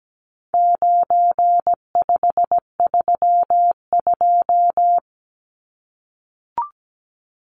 Morse code, heard as a keyed tone, about 17 wpm, 700 Hz. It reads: 9532
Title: Numbers - 4-Digit 17wpm